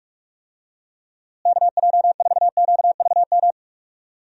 Morse code, heard as a keyed tone, about 43 wpm, 700 Hz. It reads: KJ4QVM
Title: USA Call Signs - 6 Characters - Repeated 2x - 45wpm